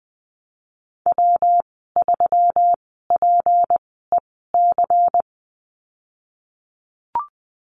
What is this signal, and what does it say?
Morse code, heard as a keyed tone, about 20 wpm, spaced as for 15 wpm, 700 Hz. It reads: W3PEC
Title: 167 Lesson 27g - Number 7 - Call Signs 20wpm